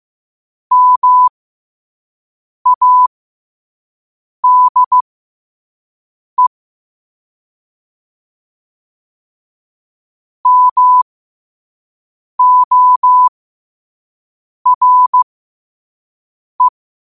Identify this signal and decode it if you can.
Morse code, heard as a keyed tone, about 15 wpm, spaced as for 5 wpm, 1000 Hz. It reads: MADE MORE
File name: qotdfiles / 5wpm